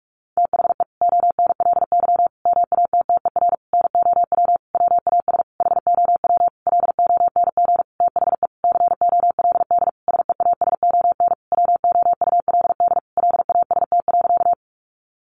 Morse code, heard as a keyed tone, about 44 wpm, 700 Hz. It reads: THE ONLY MATTER NOW WAS HOW LONG THE COLD SEASON WOULD LAST.
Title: Sentences from Top 1400 Words 45wpm